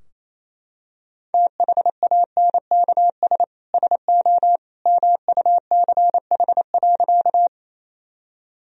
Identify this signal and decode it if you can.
Morse code, heard as a keyed tone, about 28 wpm, 700 Hz. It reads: THANKS SO MUCH.
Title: Sentences from Top 1000 Words - Review Part 3 - 28wpm